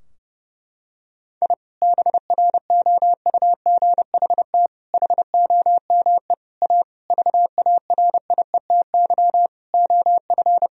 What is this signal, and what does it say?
Morse code, heard as a keyed tone, about 32 wpm, 700 Hz. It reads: I BROUGHT HOME A VARIETY OF
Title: Sentences from Top 1000 Words - Review Part 4 - 30wpm